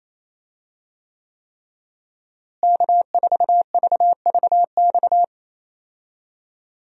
Morse code, heard as a keyed tone, about 27 wpm, 700 Hz. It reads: K4VVX
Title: USA Call Signs - 5 Characters - Repeated 3x - 28wpm